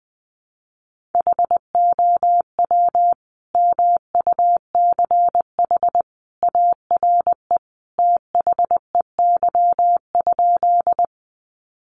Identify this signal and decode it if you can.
Morse code, heard as a keyed tone, about 20 wpm, 700 Hz. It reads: HOW MUCH ARE THEY?
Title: Sentences from Top 200 Words 20wpm